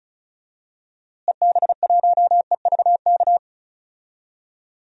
Morse code, heard as a keyed tone, about 34 wpm, 700 Hz. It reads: EB1EVK
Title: DXCC-100 Call Signs - 6 Characters - Repeated 2x - 35wpm